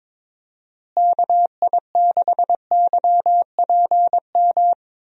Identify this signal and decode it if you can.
Morse code, heard as a keyed tone, about 22 wpm, 700 Hz. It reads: KI6YPM
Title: USA Call Signs - 6 Characters - Repeated 3x - 22wpm